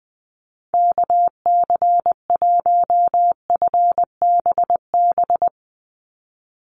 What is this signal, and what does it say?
Morse code, heard as a keyed tone, about 20 wpm, 700 Hz. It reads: KC1FBB
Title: USA Call Signs 20wpm